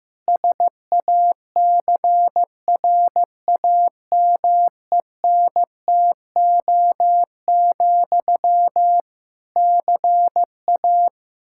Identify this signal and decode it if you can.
Morse code, heard as a keyed tone, about 15 wpm, 700 Hz. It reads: SACRAMENTO, CA
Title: Practice QSO - Exchange 1 - Beginning - 15wpm